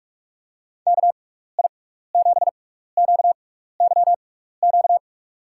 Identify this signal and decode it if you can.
Morse code, heard as a keyed tone, about 42 wpm, spaced as for 15 wpm, 700 Hz. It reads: KI7QYQ
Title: USA Call Signs - 6x Character Spacing - 45wpm